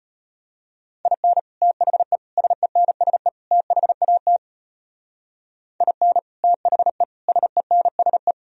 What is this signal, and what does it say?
Morse code, heard as a keyed tone, about 30 wpm, 700 Hz. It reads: IN THE SENSE THAT IN THE SENSE
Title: Speed-Racing - Sets of 4 Words 25wpm